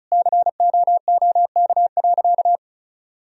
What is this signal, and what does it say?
Morse code, heard as a keyed tone, about 36 wpm, 700 Hz. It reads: COOK.
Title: Sentences from Top 1000 Words - Review Part 4 - 35wpm